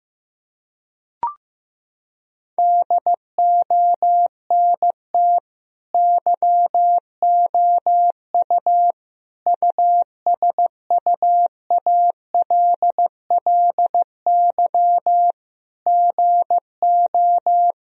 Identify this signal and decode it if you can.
Morse code, heard as a keyed tone, about 15 wpm, 700 Hz. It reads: DONT YOU USUALLY GO
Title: Sentences from Top 1000 Words - Review Part 3 - 15wpm